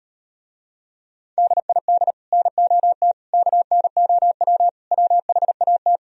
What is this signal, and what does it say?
Morse code, heard as a keyed tone, about 38 wpm, 700 Hz. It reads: DID NOT KNOW WHAT